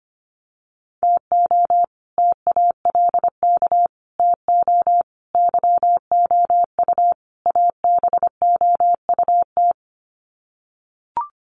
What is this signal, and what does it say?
Morse code, heard as a keyed tone, about 25 wpm, 700 Hz. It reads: TO TALK TO YOU ABOUT
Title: Sets of 5 Words Encore 25wpm